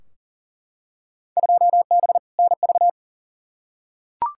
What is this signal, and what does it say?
Morse code, heard as a keyed tone, about 38 wpm, 700 Hz. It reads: 2B NV
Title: Field Day Exchanges Encore - 40wpm